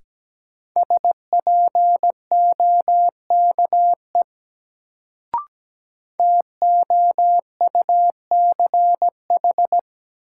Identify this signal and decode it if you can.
Morse code, heard as a keyed tone, about 17 wpm, 700 Hz. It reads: SPOKE TOUCH
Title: Sentences from Top 1200 Words - New Word Drill Alt - 17wpm